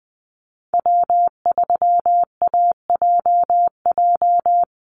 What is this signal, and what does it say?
Morse code, heard as a keyed tone, about 20 wpm, 700 Hz. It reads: W3AJJ
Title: USA Call Signs 20wpm